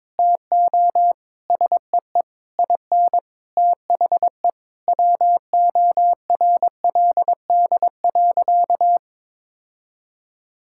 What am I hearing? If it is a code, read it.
Morse code, heard as a keyed tone, about 22 wpm, 700 Hz. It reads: TO SEE IN THE WORLD.
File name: Sentences with 9-Word Phrases - 22wpm